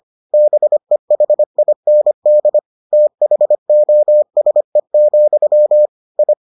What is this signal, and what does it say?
Morse code, heard as a keyed tone, about 25 wpm, 600 Hz. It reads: BEHIND THOSE, I